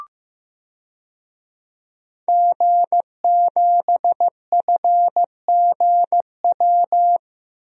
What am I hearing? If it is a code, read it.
Morse code, heard as a keyed tone, about 15 wpm, 700 Hz. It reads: G7FGW
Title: DXCC-100 Call Signs - 5 Characters - Repeated 2x - 15wpm